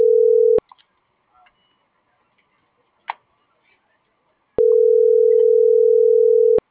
Ringback tone (North America), 440 + 480 Hz.